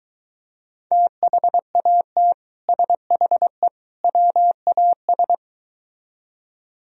Morse code, heard as a keyed tone, about 23 wpm, 700 Hz. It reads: THAT SHE WAS